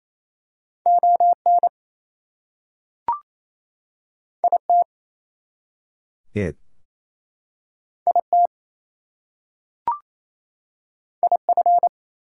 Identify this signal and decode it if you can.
Morse code, heard as a keyed tone, about 28 wpm, 700 Hz. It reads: ON IT IT IF
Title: Two Letter Words - 28wpm